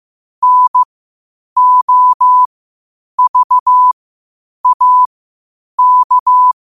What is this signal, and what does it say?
Morse code, heard as a keyed tone, about 15 wpm, spaced as for 8 wpm, 1000 Hz. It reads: NOVAK
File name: qotdfiles / 8wpm